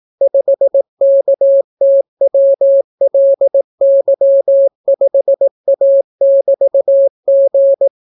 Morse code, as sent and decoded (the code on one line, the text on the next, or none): ..... -.- - .-- .-.. -.-- ..... .- -...- --.
5KTWLY5A<BT>G